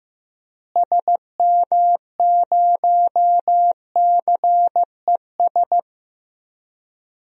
Morse code, heard as a keyed tone, about 15 wpm, 700 Hz. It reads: SM0CES
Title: DXCC-100 Call Signs - 6 Characters - Repeated 2x - 15wpm